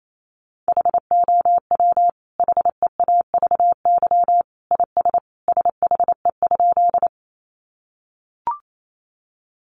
Morse code, heard as a keyed tone, about 28 wpm, 700 Hz. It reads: HOW HEAVY IS SHE?
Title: Sentences from Top 1100 Words 28wpm